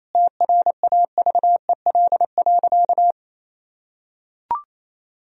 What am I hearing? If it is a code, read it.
Morse code, heard as a keyed tone, about 28 wpm, 700 Hz. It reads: TRAVEL.
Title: Sentences from Top 1000 Words - Review Part 2 - 28wpm